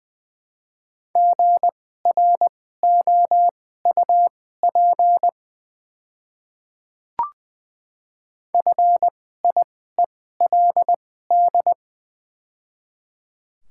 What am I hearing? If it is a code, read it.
Morse code, heard as a keyed tone, about 21 wpm, spaced as for 15 wpm, 700 Hz. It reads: GROUP FIELD
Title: 2x Character Spacing - Top 500 Words - 20wpm